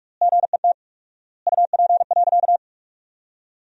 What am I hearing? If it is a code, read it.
Morse code, heard as a keyed tone, about 42 wpm, 690 Hz. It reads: GET UP.
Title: 4x Word-Spacing - Sentences from Top 500 Words - 45wpm